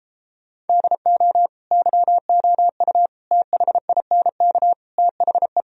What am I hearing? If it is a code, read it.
Morse code, heard as a keyed tone, about 33 wpm, 700 Hz. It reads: DO YOU THINK THE